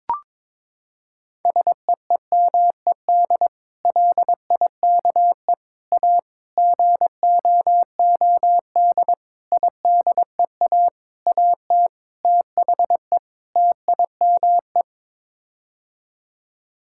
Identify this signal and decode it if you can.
Morse code, heard as a keyed tone, about 22 wpm, 700 Hz. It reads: SEEMED LIKE A GOOD IDEA AT THE TIME
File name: practice / 22wpm / Sets of 8 Words - 22wpm